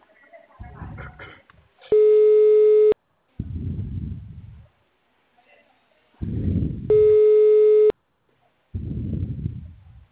Ringback tone (Europe, 425 Hz).